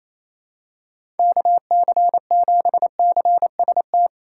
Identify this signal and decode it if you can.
Morse code, heard as a keyed tone, about 28 wpm, 700 Hz. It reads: KC7CST
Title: USA Call Signs - 6 Characters - 2x Character Spacing - 28wpm